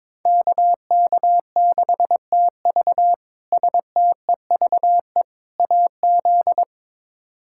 Morse code, heard as a keyed tone, about 22 wpm, 700 Hz. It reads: KK6TV STEVE AZ
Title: K1USN SST QSO Practice - 22wpm